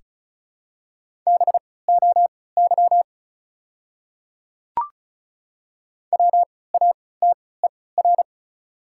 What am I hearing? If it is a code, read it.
Morse code, heard as a keyed tone, about 36 wpm, spaced as for 20 wpm, 700 Hz. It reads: BOY WATER
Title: 3x Character Spacing - Top 200 Words - 35wpm